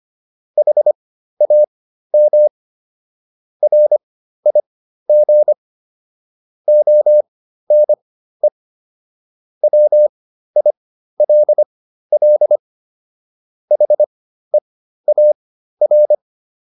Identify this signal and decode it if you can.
Morse code, heard as a keyed tone, about 25 wpm, spaced as for 13 wpm, 600 Hz. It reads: HAM RIG ONE WILL HEAR